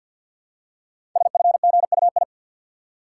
Morse code, heard as a keyed tone, about 51 wpm, 700 Hz. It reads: SFGUI
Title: Code Groups - 5 Letter:Numbers - 50wpm